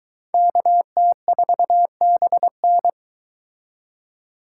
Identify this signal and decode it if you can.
Morse code, heard as a keyed tone, about 23 wpm, 700 Hz. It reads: KT4BN